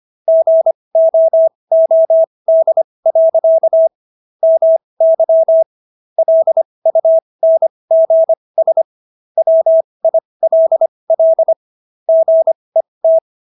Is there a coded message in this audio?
Yes: Morse code from 0.3 to 13.2 s, heard as a keyed tone, about 25 wpm, spaced as for 20 wpm, 650 Hz. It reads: GOOD. MY LUNGS WILL GET